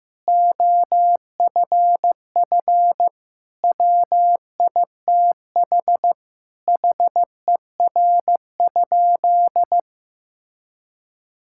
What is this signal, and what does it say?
Morse code, heard as a keyed tone, about 15 wpm, 700 Hz. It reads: OFF WITH HER?
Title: Sentences from Top 1000 Words - Review Part 6 - 15wpm